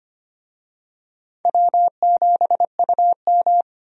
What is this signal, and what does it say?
Morse code, heard as a keyed tone, about 25 wpm, 700 Hz. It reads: W7UM
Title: USA Call Signs - 4 Characters - 25wpm